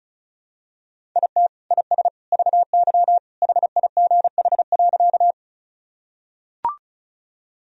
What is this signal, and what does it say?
Morse code, heard as a keyed tone, about 36 wpm, 700 Hz. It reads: IT IS VY HIGH.
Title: Sentences from Top 200 Words and Abbreviations 35wpm